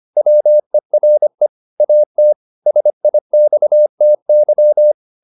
Morse code, heard as a keyed tone, about 25 wpm, 600 Hz. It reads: WERE AT SIXTY
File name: the old man and the sea part 1 25wpm